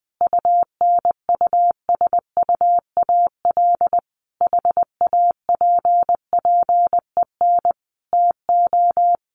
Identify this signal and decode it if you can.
Morse code, heard as a keyed tone, about 20 wpm, 700 Hz. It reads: UNUSUAL HAPPEN TO
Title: Sentences from Top 1400 Words 20wpm